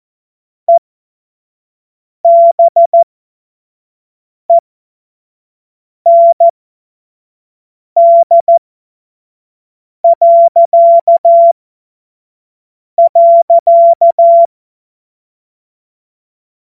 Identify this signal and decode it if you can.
Morse code, heard as a keyed tone, about 14 wpm, spaced as for 5 wpm, 680 Hz. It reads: EBEND..
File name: Morsekurs/Witze/Witz 09 Januar 2024 25 F70